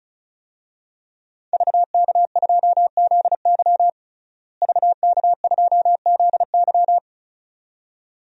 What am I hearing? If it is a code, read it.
Morse code, heard as a keyed tone, about 36 wpm, 700 Hz. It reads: VK2ZY VK2ZY
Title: DXCC-100 Call Signs - 5 Characters - Repeated 2x - 35wpm